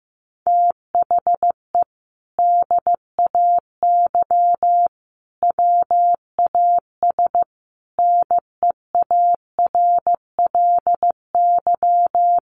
Morse code, heard as a keyed tone, about 15 wpm, 700 Hz. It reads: THE DAY WAS NEARLY
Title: Sentences from Top 1300 Words 15wpm